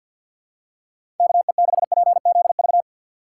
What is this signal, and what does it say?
Morse code, heard as a keyed tone, about 52 wpm, 700 Hz. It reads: KE6PZV
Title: USA Call Signs - 6 Characters - 4x Character Spacing - 50wpm